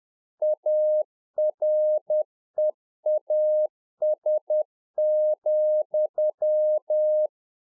Morse code, heard as a keyed tone, about 10 wpm, 610 Hz. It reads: AREAS,